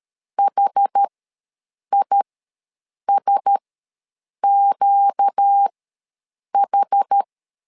Morse code, heard as a keyed tone, about 13 wpm, spaced as for 7 wpm, 800 Hz. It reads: HISQH